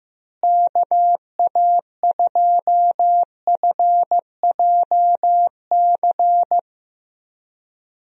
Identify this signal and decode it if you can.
Morse code, heard as a keyed tone, about 16 wpm, 700 Hz. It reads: KA2FJC
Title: USA Call Signs - 6 Characters - Repeated 2x - 15wpm